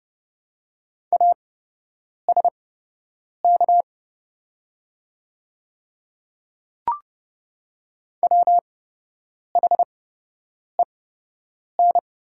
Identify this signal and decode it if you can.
Morse code, heard as a keyed tone, about 30 wpm, spaced as for 8 wpm, 700 Hz. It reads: ASK WHEN
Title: 8x Character Spacing - Top 200 Words - 30wpm